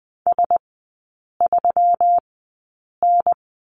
Morse code, heard as a keyed tone, about 20 wpm, spaced as for 8 wpm, 700 Hz. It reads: S3N
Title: 127 Lesson 21e - Number 6 - Three Characters 20wpm